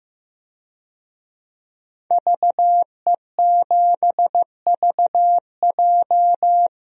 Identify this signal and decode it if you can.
Morse code, heard as a keyed tone, about 15 wpm, 700 Hz. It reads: VE7VJ
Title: DXCC-100 Call Signs - 5 Characters - 15wpm